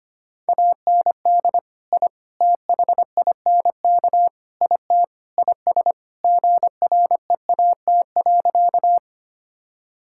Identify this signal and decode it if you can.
Morse code, heard as a keyed tone, about 25 wpm, 700 Hz. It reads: AND I THINK IT IS GREAT.
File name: practice / 25wpm / Sentences with 4-Word Phrases - 25wpm